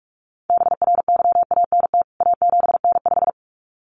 Morse code, heard as a keyed tone, about 44 wpm, 700 Hz. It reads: BRYANT A7N5
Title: CWT Exchange 45wpm